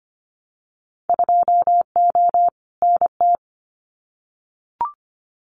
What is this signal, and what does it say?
Morse code, heard as a keyed tone, about 25 wpm, 700 Hz. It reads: 2O NT
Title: Winter Field Day Exchanges Encore 25wpm